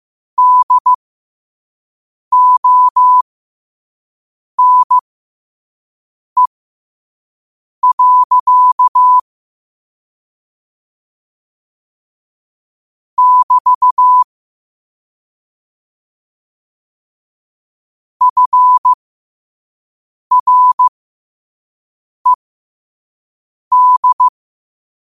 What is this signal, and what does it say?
Morse code, heard as a keyed tone, about 15 wpm, spaced as for 5 wpm, 1000 Hz. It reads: DONE. <BT> FRED
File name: qotdfiles / 5wpm